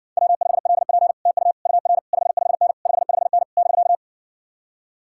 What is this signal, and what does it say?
Morse code, heard as a keyed tone, about 61 wpm, 700 Hz. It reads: W5LP TU UR 55N 55N <BK>